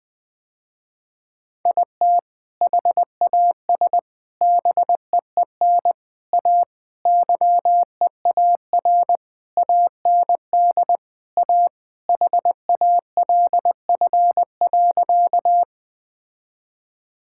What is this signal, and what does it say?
Morse code, heard as a keyed tone, about 20 wpm, 700 Hz. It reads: IT HAS BEEN A YEAR AND A HALF.
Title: Sentences with 5-Word Phrases - Spoken Beginning - 20wpm